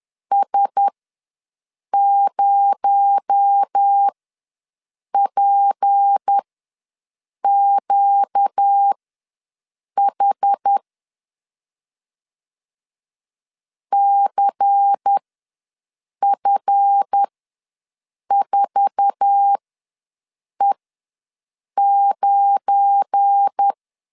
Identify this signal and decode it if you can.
Morse code, heard as a keyed tone, about 10 wpm, spaced as for 6 wpm, 800 Hz. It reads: S0PQH CF4E9
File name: LEZIONI / 8 YJ45 / 8 YJ4 ws 10wpm 4020